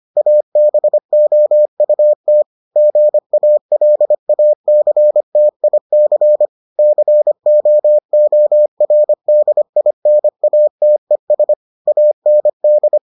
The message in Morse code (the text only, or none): ABOUT GALACTIC COORDINATES AND